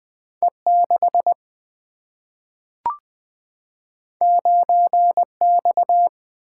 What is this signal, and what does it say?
Morse code, heard as a keyed tone, about 20 wpm, 700 Hz. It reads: E6 9X
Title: Single-Letter Single-Number 20wpm